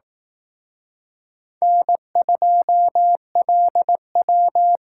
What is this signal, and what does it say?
Morse code, heard as a keyed tone, about 18 wpm, 700 Hz. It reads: N2LW
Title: Speed Racing - USA Call Signs 15wpm